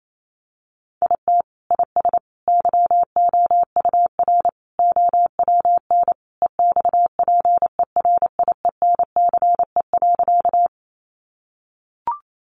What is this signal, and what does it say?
Morse code, heard as a keyed tone, about 28 wpm, 700 Hz. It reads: IT IS YOUR OWN EXPERIENCE.
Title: Sentences from Top 600 Words 28wpm